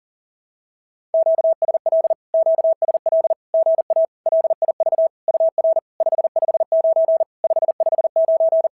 Morse code, heard as a keyed tone, about 40 wpm, 650 Hz. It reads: QSL QSL GA LIV UR 559 559
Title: SOTA QSO Practice 40wpm